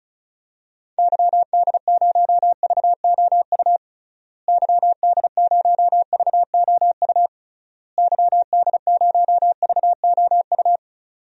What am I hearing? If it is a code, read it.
Morse code, heard as a keyed tone, about 34 wpm, 700 Hz. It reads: YD0VOU YD0VOU YD0VOU
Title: DXCC-100 Call Signs - 6 Characters - Repeated 3x - 35wpm